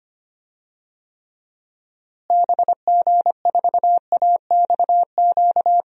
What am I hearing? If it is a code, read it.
Morse code, heard as a keyed tone, about 25 wpm, 700 Hz. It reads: BG4AXQ
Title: DXCC-100 Call Signs - 6 Characters - Repeated 2x - 25wpm